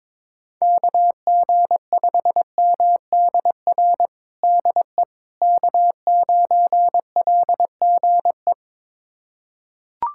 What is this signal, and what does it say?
Morse code, heard as a keyed tone, about 22 wpm, 700 Hz. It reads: KG5MDR DE K9LGE
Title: USA Callsign DE Callsign 22wpm